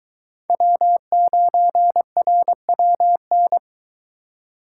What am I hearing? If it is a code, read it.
Morse code, heard as a keyed tone, about 22 wpm, 700 Hz. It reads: W9RWN